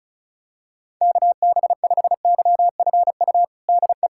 Morse code, heard as a keyed tone, about 36 wpm, 700 Hz. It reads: KB5YFU DE
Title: USA Callsign DE Callsign 35wpm